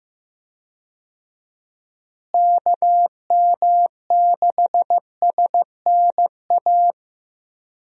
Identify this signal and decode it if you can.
Morse code, heard as a keyed tone, about 15 wpm, 700 Hz. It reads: KM6SNA
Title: USA Call Signs - 6 Characters - Repeated 3x - 15wpm